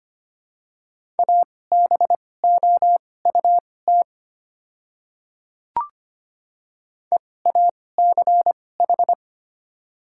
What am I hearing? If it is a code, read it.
Morse code, heard as a keyed tone, about 25 wpm, spaced as for 18 wpm, 700 Hz. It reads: ABOUT EACH
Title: 2x Character Spacing - Top 100 Words - 25wpm